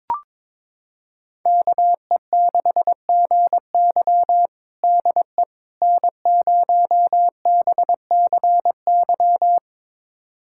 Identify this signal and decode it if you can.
Morse code, heard as a keyed tone, about 21 wpm, 700 Hz. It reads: KE6GY DE N0BCY
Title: USA Callsign DE Callsign 22wpm